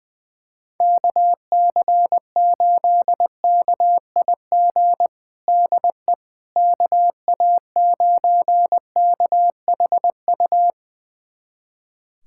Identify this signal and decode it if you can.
Morse code, heard as a keyed tone, about 20 wpm, 700 Hz. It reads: KC8KIG DE KA9KHU
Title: USA Callsign DE Callsign - 6 Characters - 20wpm